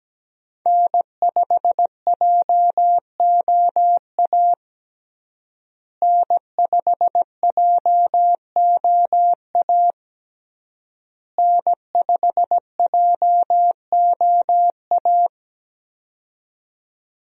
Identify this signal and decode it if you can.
Morse code, heard as a keyed tone, about 17 wpm, 700 Hz. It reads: N5JOA N5JOA N5JOA
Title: USA Call Signs - 5 Characters - Repeated 3x - 17wpm